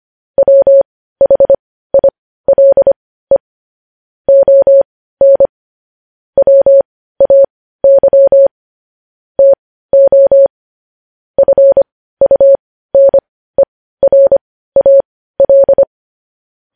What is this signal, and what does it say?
Morse code, heard as a keyed tone, about 25 wpm, spaced as for 15 wpm, 550 Hz. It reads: WHILE ON WAY TO FUNERAL
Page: news-headlines-in-morse-code-at-15-wpm